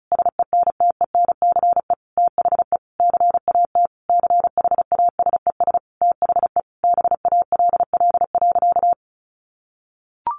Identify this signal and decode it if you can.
Morse code, heard as a keyed tone, about 34 wpm, 700 Hz. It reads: SENTENCE THE CAT CHASES THE BALL.
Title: Additional Easy Phrases 35wpm